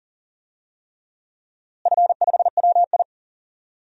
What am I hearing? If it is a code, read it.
Morse code, heard as a keyed tone, about 40 wpm, 700 Hz. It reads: F5WI
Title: DXCC-100 Call Signs - 4 Characters - Repeated 3x - 40wpm